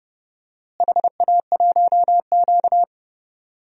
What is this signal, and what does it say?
Morse code, heard as a keyed tone, about 30 wpm, 700 Hz. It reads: HA1Q
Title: DXCC-100 Call Signs - 4 Characters - Repeated 3x - 30wpm